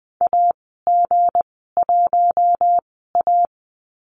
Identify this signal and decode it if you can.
Morse code, heard as a keyed tone, about 20 wpm, spaced as for 15 wpm, 700 Hz. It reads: AG1A
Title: 186 Lesson 30g - Letter b - Call Signs 20wpm